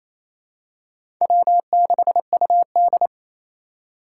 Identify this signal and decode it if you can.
Morse code, heard as a keyed tone, about 28 wpm, 700 Hz. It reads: W6UD
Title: USA Call Signs - 4 Characters - 4x Character Spacing - 28wpm